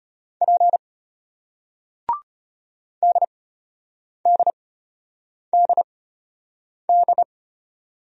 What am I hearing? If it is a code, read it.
Morse code, heard as a keyed tone, about 33 wpm, spaced as for 8 wpm, 700 Hz. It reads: P DDDD